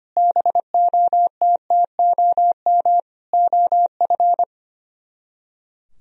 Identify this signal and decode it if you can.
Morse code, heard as a keyed tone, about 31 wpm, 700 Hz. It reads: BOTTOM OF
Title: Speed-Racing - Sets of 3 Words 25wpm